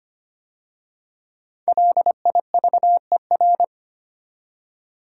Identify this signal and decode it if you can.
Morse code, heard as a keyed tone, about 25 wpm, 700 Hz. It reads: LIVER
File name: 5 Letter Words - First 3 Letters Given - 25wpm